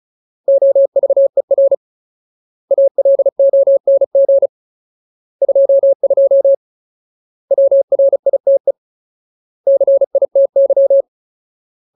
Morse code, heard as a keyed tone, about 34 wpm, 550 Hz. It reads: OVER ALONG 22 WRITE CITY
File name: Ultimate Word List / 35WPM Ultimate